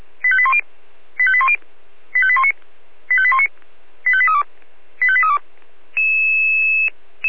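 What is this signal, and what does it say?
Five-tone selective call (ZVEI), heard as quick runs of five tones, tones of about 70 ms. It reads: 85810 85810 85819 85819 85832 85832
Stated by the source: Received by SDR.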